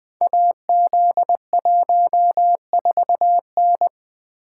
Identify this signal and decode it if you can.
Morse code, heard as a keyed tone, about 20 wpm, 700 Hz. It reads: AZ14N
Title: Code Groups - 5 Letter:Numbers - 20wpm